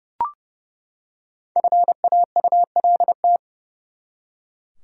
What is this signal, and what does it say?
Morse code, heard as a keyed tone, about 30 wpm, 700 Hz. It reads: FAULT
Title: Five Letter Words - 30wpm